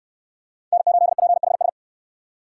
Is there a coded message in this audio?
Yes: Morse code from 0.7 to 1.7 s, heard as a keyed tone, about 67 wpm, 700 Hz. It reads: N8PHS